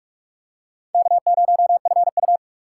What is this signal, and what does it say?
Morse code, heard as a keyed tone, about 46 wpm, 700 Hz. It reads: K0FU
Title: USA Call Signs - 4x Character Spacing - 45wpm